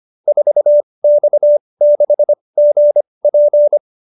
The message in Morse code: 4X6GP

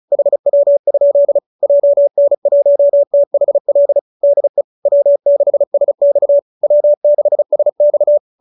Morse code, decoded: HW? JN1THL DE W6SX W6SX

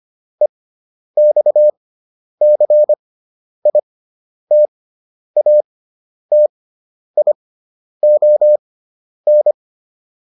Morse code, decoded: EXCITATION

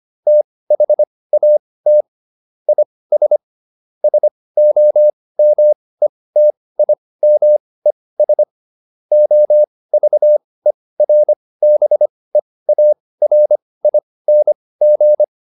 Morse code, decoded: THAT IS SOMETIMES OVERBEARING

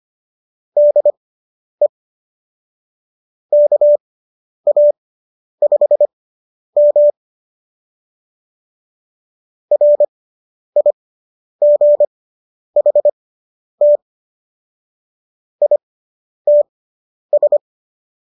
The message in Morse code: DE KA5M RIGHT ITS

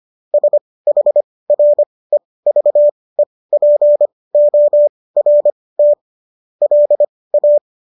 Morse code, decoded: SHREVEPORT LA

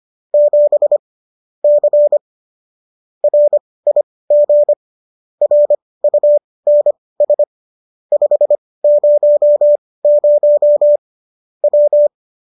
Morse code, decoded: 7 C RIG RUNS 500 W

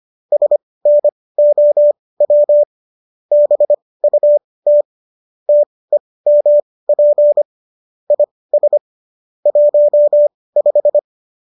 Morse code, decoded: SNOW BUT TEMP IS 15

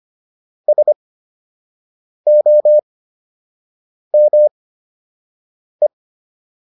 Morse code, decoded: SOME

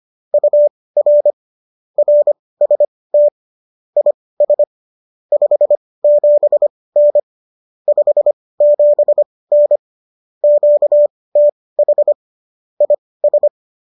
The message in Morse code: UR RST IS 57N 57N QTH IS

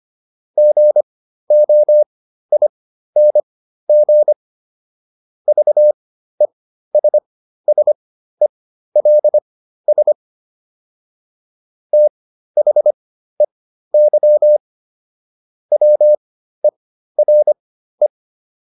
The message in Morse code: GOING VESSELS THEY WERE

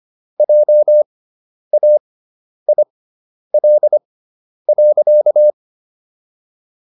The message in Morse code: JAIL.

